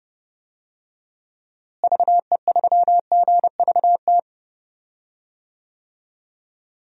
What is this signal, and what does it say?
Morse code, heard as a keyed tone, about 32 wpm, 700 Hz. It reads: VE3GVT
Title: DXCC-100 Call Signs - 6 Characters - Repeated 3x - 30wpm